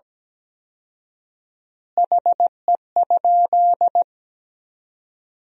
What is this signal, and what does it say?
Morse code, heard as a keyed tone, about 17 wpm, 700 Hz. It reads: HE?
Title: 4x Word-Spacing - Sentences from Top 100 Words - 17wpm